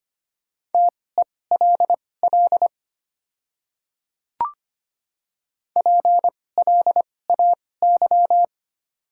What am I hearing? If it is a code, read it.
Morse code, heard as a keyed tone, about 25 wpm, spaced as for 18 wpm, 700 Hz. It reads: TELL PLAY
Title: 2x Character Spacing - Top 200 Words - 25wpm